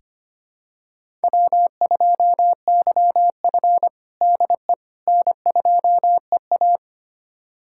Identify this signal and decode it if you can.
Morse code, heard as a keyed tone, about 25 wpm, 700 Hz. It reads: W2YF DE N2EA
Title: USA Callsign DE Callsign - 4 Characters - 25wpm